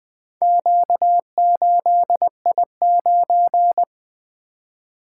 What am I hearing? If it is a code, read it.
Morse code, heard as a keyed tone, about 25 wpm, 700 Hz. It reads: Q8I9
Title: Speed-Racing - Four Letter-Number Combinations - 20wpm